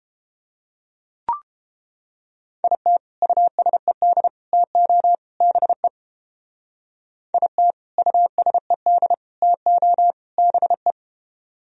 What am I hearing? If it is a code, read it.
Morse code, heard as a keyed tone, about 29 wpm, 700 Hz. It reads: IT USED TO BE IT USED TO BE